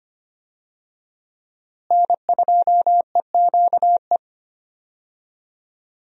Morse code, heard as a keyed tone, about 25 wpm, 700 Hz. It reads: N2EQE
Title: USA Call Signs - 5 Characters - Repeated 2x - 25wpm